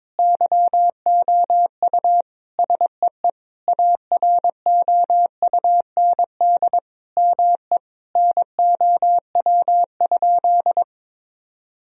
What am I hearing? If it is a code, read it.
Morse code, heard as a keyed tone, about 21 wpm, 700 Hz. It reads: YOU SEE AROUND ME NOW?
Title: Sentences from Top 1000 Words - Review Part 3 - 22wpm